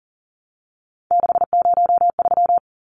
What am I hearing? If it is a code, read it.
Morse code, heard as a keyed tone, about 40 wpm, 700 Hz. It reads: 603
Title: Numbers - 3-Digit 40wpm